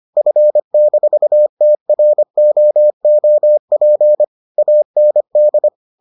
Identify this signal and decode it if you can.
Morse code, heard as a keyed tone, about 25 wpm, 600 Hz. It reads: F-TROOP AND